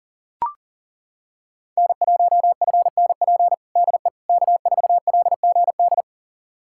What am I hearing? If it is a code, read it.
Morse code, heard as a keyed tone, about 40 wpm, 700 Hz. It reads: N1FNP DE K4LGD
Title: USA Callsign DE Callsign - 5 Characters - 40wpm